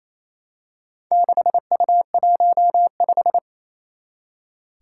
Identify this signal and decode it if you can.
Morse code, heard as a keyed tone, about 27 wpm, 700 Hz. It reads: 6U15
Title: Four Challenging Letter-Number Combinations 28wpm